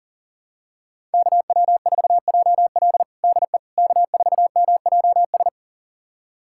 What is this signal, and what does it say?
Morse code, heard as a keyed tone, about 40 wpm, 700 Hz. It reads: KW4JL DE K4MJS
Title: USA Callsign DE Callsign - 5 Characters - 40wpm